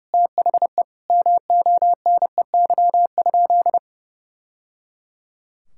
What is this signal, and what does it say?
Morse code, heard as a keyed tone, about 30 wpm, 700 Hz. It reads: THE MONEY?
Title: Sentences from Top 1000 Words - Review Part 5 - 30wpm